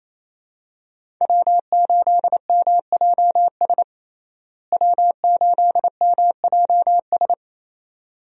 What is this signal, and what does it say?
Morse code, heard as a keyed tone, about 28 wpm, 700 Hz. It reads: W8MJS W8MJS
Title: USA Call Signs - 5 Characters - Repeated 2x - 28wpm